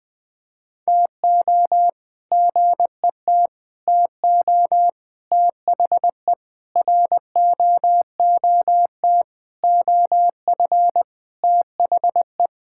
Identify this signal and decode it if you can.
Morse code, heard as a keyed tone, about 21 wpm, 700 Hz. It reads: TO GET TO THE ROOT OF THE
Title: All-Age Appropriate Jokes - 20wpm